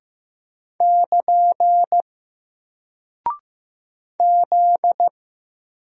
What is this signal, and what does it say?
Morse code, heard as a keyed tone, about 15 wpm, 700 Hz. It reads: <KN> Z
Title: Single Character (Letter:Number:Punctuation:Prosign) Alt 15wpm